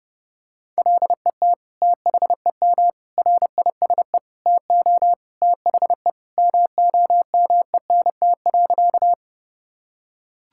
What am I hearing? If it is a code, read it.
Morse code, heard as a keyed tone, about 32 wpm, 700 Hz. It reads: LET THEM RISE TO THE MOMENT.
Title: Sentences from Top 1000 Words - Review Part 6 - 30wpm